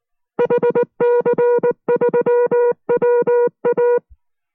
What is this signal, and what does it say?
Morse code, heard as a keyed tone, about 19 wpm, 480 Hz. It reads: HC3WA